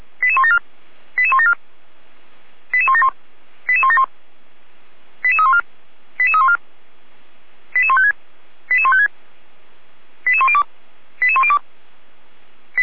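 Five-tone selective call (ZVEI), heard as quick runs of five tones, tones of about 80 ms. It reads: 88174 88174 88171 88171 88314 88314 88156 88156 88102 88102
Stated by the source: ZVEI five-tone call received by SDR, Feuerwehr Orlamünde.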